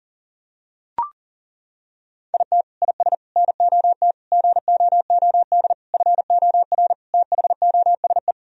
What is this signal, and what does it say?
Morse code, heard as a keyed tone, about 40 wpm, 700 Hz. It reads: IT IS NOT GOOD FOR THOSE
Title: Sentences from Top 1000 Words - Review Part 2 - 40wpm